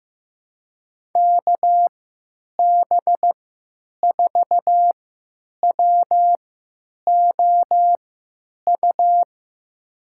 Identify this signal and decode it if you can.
Morse code, heard as a keyed tone, about 15 wpm, spaced as for 9 wpm, 700 Hz. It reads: KB4WOU
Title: USA Call Signs - 6 Characters - 3x Character Spacing - 15wpm